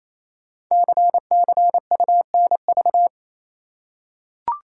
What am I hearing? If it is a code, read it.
Morse code, heard as a keyed tone, about 27 wpm, 700 Hz. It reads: CCUNV
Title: Code Groups - 5 Letters - 28wpm